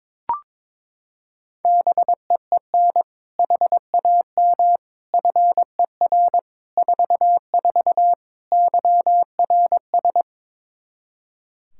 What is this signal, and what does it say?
Morse code, heard as a keyed tone, about 22 wpm, 700 Hz. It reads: BEEN HAM FER 44 YRS
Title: Practice QSO - Exchange 3 - More Info - 22wpm